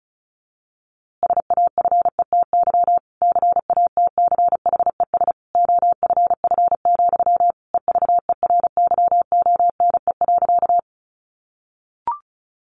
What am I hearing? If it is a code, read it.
Morse code, heard as a keyed tone, about 34 wpm, 700 Hz. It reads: SAFETY CATCHES OFF, EVERYONE.
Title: A Sound Of Thunder 35wpm